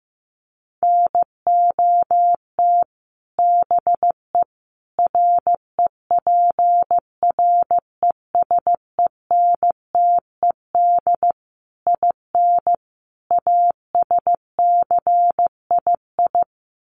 Morse code, heard as a keyed tone, about 15 wpm, 700 Hz. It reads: NOT BE REPRESENTED IN ASCII
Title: Sets of 5 Words 15wpm